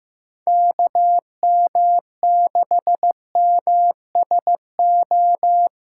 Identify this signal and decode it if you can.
Morse code, heard as a keyed tone, about 16 wpm, 700 Hz. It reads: KM6MSO
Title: USA Call Signs - 6 Characters - 6x Character Spacing - 15wpm